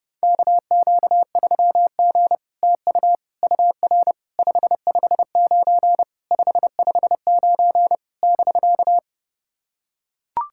Morse code, heard as a keyed tone, about 30 wpm, 700 Hz. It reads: KQ3G TU UR 559 559 <BK>
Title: SOTA QSO Practice 30wpm